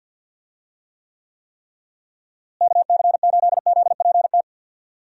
Morse code, heard as a keyed tone, about 51 wpm, 700 Hz. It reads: KC8ZPT
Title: USA Call Signs - 6 Characters - Repeated 3x - 50wpm